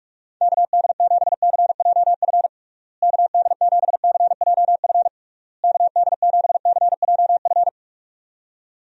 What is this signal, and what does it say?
Morse code, heard as a keyed tone, about 46 wpm, 700 Hz. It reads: KD7CJF KD7CJF KD7CJF
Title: USA Call Signs - 6 Characters - Repeated 3x - 45wpm